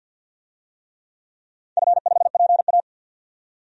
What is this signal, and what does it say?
Morse code, heard as a keyed tone, about 51 wpm, 700 Hz. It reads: F5PA
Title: DXCC-100 Call Signs - 4 Characters - 50wpm